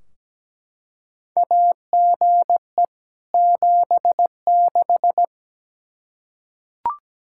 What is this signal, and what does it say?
Morse code, heard as a keyed tone, about 17 wpm, 700 Hz. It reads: AGE 76
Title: Practice QSO - Exchange 3 - More Info - 17wpm